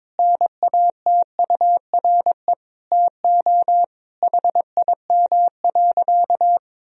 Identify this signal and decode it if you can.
Morse code, heard as a keyed tone, about 22 wpm, 700 Hz. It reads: NATURE TO HIM.
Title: Sentences from Top 1000 Words - Review Part 1 - 22wpm